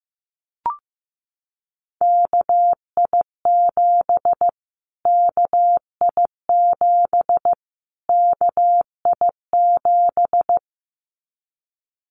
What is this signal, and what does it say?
Morse code, heard as a keyed tone, about 15 wpm, 700 Hz. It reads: KI7 KI7 KI7
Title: Callsign Prefixes for Top 25 DXCC - Repeated 3x - 15wpm